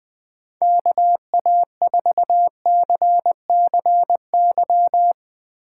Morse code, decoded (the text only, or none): KA4CCY